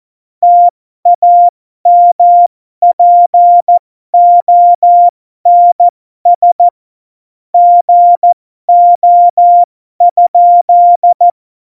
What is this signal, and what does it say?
Morse code, heard as a keyed tone, about 14 wpm, spaced as for 12 wpm, 700 Hz. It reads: TAMPONS GO?
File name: Morsekurs / Witze